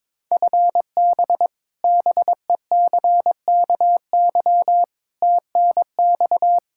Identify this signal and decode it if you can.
Morse code, heard as a keyed tone, about 22 wpm, 700 Hz. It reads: FB BECKY TNX
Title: Practice QSO - Exchange 3 - More Info - 22wpm